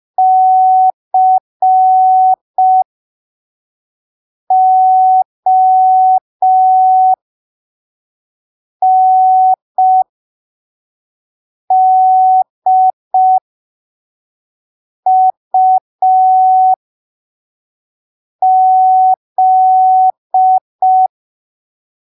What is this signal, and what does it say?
Morse code, heard as a keyed tone, about 5 wpm, spaced as for 3 wpm, 750 Hz. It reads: CONDUZ
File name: arquivos / texto / texto 05 5ppm